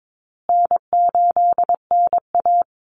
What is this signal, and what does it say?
Morse code, heard as a keyed tone, about 22 wpm, 700 Hz. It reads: N8NA
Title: Call Signs Repeated 2x - 22wpm